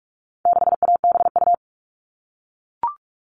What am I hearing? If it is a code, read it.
Morse code, heard as a keyed tone, about 44 wpm, 700 Hz. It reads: 6ABU
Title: All Cut Numbers - 4-Digit 45wpm